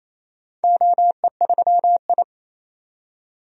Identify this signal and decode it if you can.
Morse code, heard as a keyed tone, about 28 wpm, 700 Hz. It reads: OE3I
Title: DXCC-100 Call Signs - 4 Characters - 28wpm